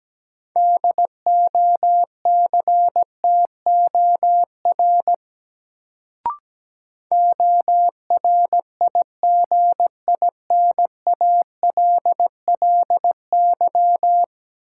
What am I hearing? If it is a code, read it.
Morse code, heard as a keyed tone, about 17 wpm, 700 Hz. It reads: DOCTOR ORIGINALLY
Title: Sentences from Top 1500 Words - New Word Drill - 17wpm